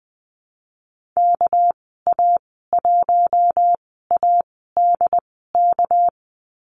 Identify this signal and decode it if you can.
Morse code, heard as a keyed tone, about 20 wpm, spaced as for 14 wpm, 700 Hz. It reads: KA1ADK
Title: 213 Lesson 34g - Number 8 - Call Signs 20wpm